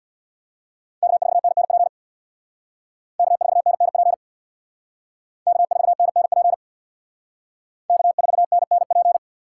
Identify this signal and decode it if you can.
Morse code, heard as a keyed tone, about 64 wpm, 700 Hz. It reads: K4NNP K4NNP K4NNP K4NNP